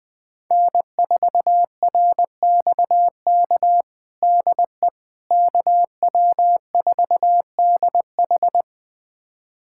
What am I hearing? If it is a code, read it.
Morse code, heard as a keyed tone, about 21 wpm, 700 Hz. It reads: N4RXK DE KW4DH
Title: USA Callsign DE Callsign - 5 Characters - 20wpm